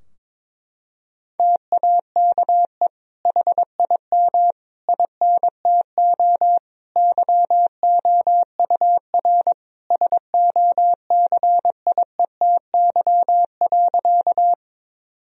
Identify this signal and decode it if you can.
Morse code, heard as a keyed tone, about 22 wpm, 700 Hz. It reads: TAKE HIM INTO YOUR SOCIETY.
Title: Sentences from Top 1000 Words - Review Part 4 - 22wpm